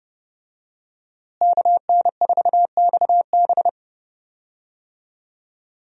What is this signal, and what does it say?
Morse code, heard as a keyed tone, about 30 wpm, 700 Hz. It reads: KN4XB
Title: USA Call Signs - 5 Characters - Repeated 3x - 30wpm